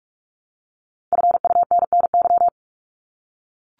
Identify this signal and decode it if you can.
Morse code, heard as a keyed tone, about 44 wpm, 700 Hz. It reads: FUNNY
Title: Top 1751-2000 Words 45wpm